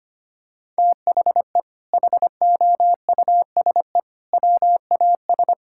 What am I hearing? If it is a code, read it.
Morse code, heard as a keyed tone, about 25 wpm, 700 Hz. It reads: THE HOUSE WAS
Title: Sentences from Top 1000 Words - Review Part 6 - 25wpm